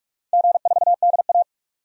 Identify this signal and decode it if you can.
Morse code, heard as a keyed tone, about 41 wpm, 700 Hz. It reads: G4DA
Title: DXCC-100 Call Signs - 4 Characters - 45wpm